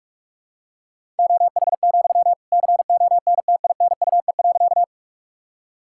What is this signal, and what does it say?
Morse code, heard as a keyed tone, about 42 wpm, 700 Hz. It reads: OH, CONTINUE.
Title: Sentences from Top 1000 Words - Review Part 6 - 45wpm